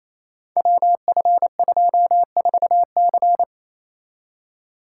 Morse code, heard as a keyed tone, about 28 wpm, 700 Hz. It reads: WF24C
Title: Code Groups - 5 Letter:Numbers - 28wpm